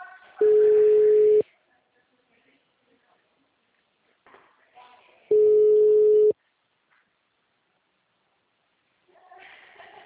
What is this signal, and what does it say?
Ringback tone (Europe, 425 Hz), with speech.
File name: out- -2041- - - .53000